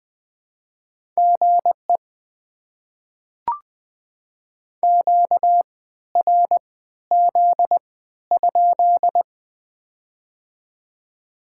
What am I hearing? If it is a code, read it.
Morse code, heard as a keyed tone, about 20 wpm, spaced as for 13 wpm, 700 Hz. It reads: GE QRZ?
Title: Top 90 QSO Elements - 3x Character Spacing - 20wpm